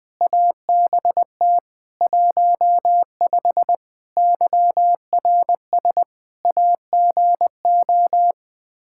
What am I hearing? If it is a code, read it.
Morse code, heard as a keyed tone, about 21 wpm, 700 Hz. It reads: ABT 15 YRS AGO
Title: Practice QSO - Exchange 3 - More Info - 20wpm